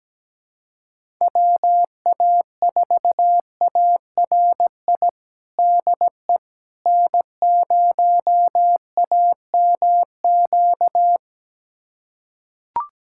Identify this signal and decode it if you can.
Morse code, heard as a keyed tone, about 17 wpm, 700 Hz. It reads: WA4ARI DE N0AMQ
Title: USA Callsign DE Callsign 17wpm